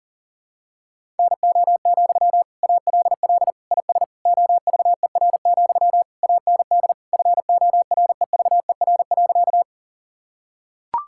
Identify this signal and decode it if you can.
Morse code, heard as a keyed tone, about 40 wpm, 700 Hz. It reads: NO, ALL IS OVER, AND FOREVER.
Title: Sentences from Top 1600 Words 40wpm